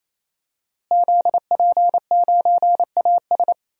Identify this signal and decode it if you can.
Morse code, heard as a keyed tone, about 28 wpm, 700 Hz. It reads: ZP9AS
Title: DXCC-100 Call Signs - 5 Characters - Repeated 3x - 28wpm